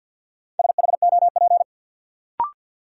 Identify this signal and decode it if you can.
Morse code, heard as a keyed tone, about 46 wpm, 700 Hz. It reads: SHOP
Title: Sentences from Top 1600 Words - New Word Drill - 50wpm